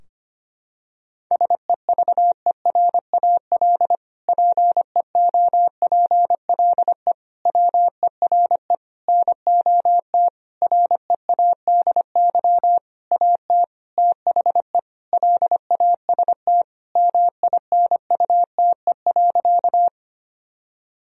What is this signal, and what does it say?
Morse code, heard as a keyed tone, about 25 wpm, 700 Hz. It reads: SEVERAL PEOPLE WERE NOT READY AT THE LAST MINUTE.
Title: Sentences from Top 1000 Words - Review Part 2 - 25wpm